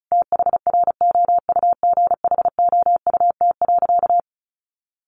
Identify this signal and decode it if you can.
Morse code, heard as a keyed tone, about 34 wpm, 700 Hz. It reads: THROUGHOUT.
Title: Sentences from Top 800 Words 35wpm